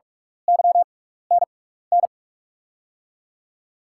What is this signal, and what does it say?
Morse code, heard as a keyed tone, about 43 wpm, spaced as for 15 wpm, 700 Hz. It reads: YNN
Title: USA Call Signs - 6 Characters - 6x Character Spacing - 45wpm